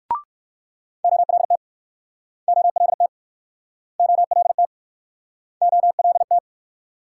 Morse code, heard as a keyed tone, about 62 wpm, 690 Hz. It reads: OLT OLT OLT OLT